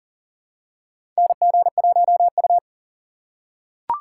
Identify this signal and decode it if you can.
Morse code, heard as a keyed tone, about 40 wpm, 700 Hz. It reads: NG1U
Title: USA Call Signs - 4 Characters - 4x Character Spacing - 40wpm